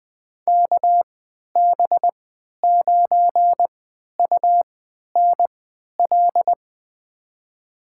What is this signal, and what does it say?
Morse code, heard as a keyed tone, about 20 wpm, spaced as for 11 wpm, 700 Hz. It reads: KB9UNL
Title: USA Call Signs - 6 Characters - 3x Character Spacing - 20wpm